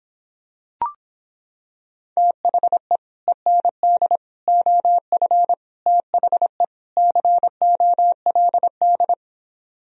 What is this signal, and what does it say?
Morse code, heard as a keyed tone, about 26 wpm, 700 Hz. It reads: THE END OF THE COLD